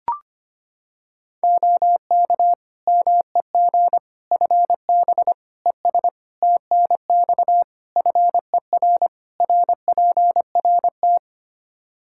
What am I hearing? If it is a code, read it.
Morse code, heard as a keyed tone, about 25 wpm, 700 Hz. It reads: OK MEG FB ES TNX FER RPRT
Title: Practice QSO - Exchange 2 - Info - 25wpm